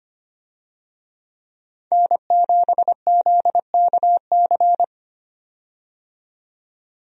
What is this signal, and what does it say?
Morse code, heard as a keyed tone, about 25 wpm, 700 Hz. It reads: N7ZKC
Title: USA Call Signs - 5 Characters - 25wpm